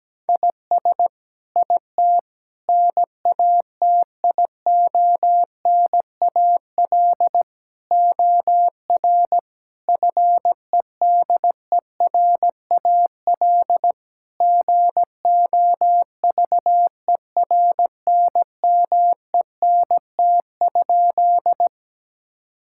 Morse code, heard as a keyed tone, about 17 wpm, 700 Hz. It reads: IS IT NATIONAL OR FEDERAL GOVERNMENT?
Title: Sentences from Top 1000 Words - Review Part 5 - 17wpm